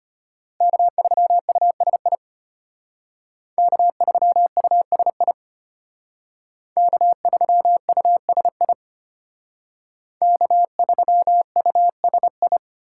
Morse code, heard as a keyed tone, about 33 wpm, 700 Hz. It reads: K3USI K3USI K3USI K3USI